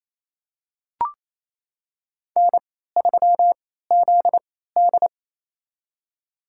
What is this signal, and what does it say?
Morse code, heard as a keyed tone, about 27 wpm, spaced as for 16 wpm, 700 Hz. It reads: N3ZD
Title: USA Call Signs - 3x Character Spacing - 28wpm